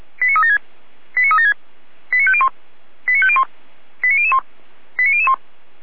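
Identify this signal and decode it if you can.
Five-tone selective call (ZVEI), heard as quick runs of five tones, tones of about 70 ms. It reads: 89376 89376 89501 89501 89001 89001
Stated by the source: ZVEI five-tone selective call received by SDR, from fire brigade Feuerwehr Orlamünde audio server.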